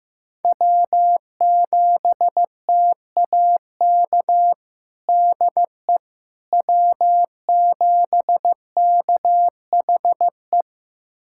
Morse code, heard as a keyed tone, about 16 wpm, 700 Hz. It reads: W7TAK DE W7KHE
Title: USA Callsign DE Callsign - 5 Characters - 15wpm